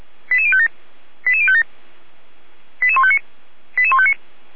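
Five-tone selective call (ZVEI), heard as quick runs of five tones, tones of about 70 ms. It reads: 88057 88057 88159 88159
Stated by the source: ZVEI five-tone call 88106 received by SDR.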